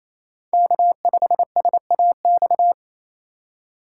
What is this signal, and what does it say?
Morse code, heard as a keyed tone, about 28 wpm, 700 Hz. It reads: K5SAX
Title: USA Call Signs - 5 Characters - 2x Character Spacing - 28wpm